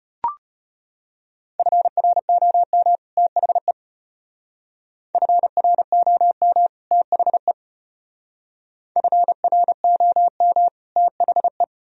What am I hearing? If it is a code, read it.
Morse code, heard as a keyed tone, about 30 wpm, 700 Hz. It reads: FROM THE FROM THE FROM THE
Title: Speed-Racing - Sets of 2 Words 25wpm